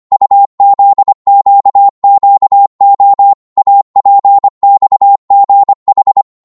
Morse code, heard as a keyed tone, about 25 wpm, 800 Hz. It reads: UZQQO APXGH